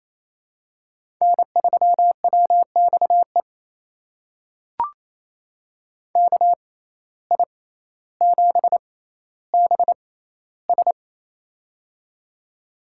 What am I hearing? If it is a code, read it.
Morse code, heard as a keyed tone, about 28 wpm, spaced as for 14 wpm, 700 Hz. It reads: N3WXE KI7BS
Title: USA Call Signs - 5 Characters - 6x Character Spacing - 28wpm